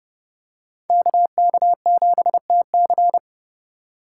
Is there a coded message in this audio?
Yes: Morse code from 0.9 to 3.2 s, heard as a keyed tone, about 29 wpm, 700 Hz. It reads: KK7TC